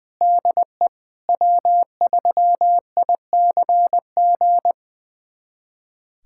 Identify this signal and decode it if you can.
Morse code, heard as a keyed tone, about 21 wpm, 700 Hz. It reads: DE W3ICG
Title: USA Callsign DE Callsign - 5 Characters - 20wpm